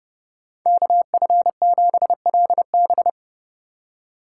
Morse code, heard as a keyed tone, about 29 wpm, 700 Hz. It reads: KF7LB